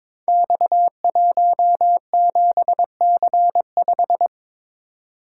Morse code, heard as a keyed tone, about 22 wpm, 700 Hz. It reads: X17C5
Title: Code Groups - 5 Letter:Numbers - 22wpm